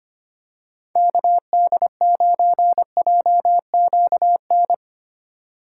Morse code, heard as a keyed tone, about 25 wpm, 700 Hz. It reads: KD9JQN
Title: USA Call Signs - 6 Characters - 3x Character Spacing - 25wpm